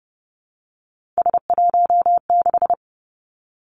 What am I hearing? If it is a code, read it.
Morse code, heard as a keyed tone, about 31 wpm, 700 Hz. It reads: S16
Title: Three Challenging Letter-Number Combinations 30wpm